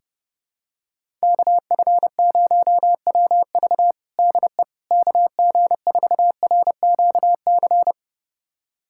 Morse code, heard as a keyed tone, about 30 wpm, 700 Hz. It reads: KF0WV DE KG4RQC
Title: USA Callsign DE Callsign 30wpm